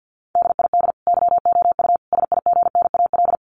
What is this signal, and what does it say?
Morse code, heard as a keyed tone, about 50 wpm, 700 Hz. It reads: DID YOU SIGNAL
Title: Sentences from Top 1400 Words 50wpm